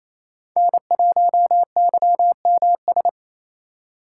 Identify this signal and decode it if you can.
Morse code, heard as a keyed tone, about 28 wpm, 700 Hz. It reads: N1YMS
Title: USA Call Signs - 5 Characters - 8x Character Spacing - 28wpm